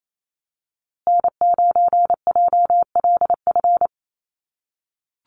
Morse code, heard as a keyed tone, about 28 wpm, 700 Hz. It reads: N9JLF
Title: USA Call Signs Repeated 2x 28wpm